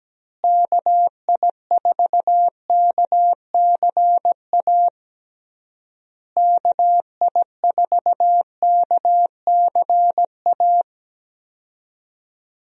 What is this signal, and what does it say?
Morse code, heard as a keyed tone, about 17 wpm, 700 Hz. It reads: KI4KCA KI4KCA
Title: USA Call Signs - 6 Characters - Repeated 3x - 17wpm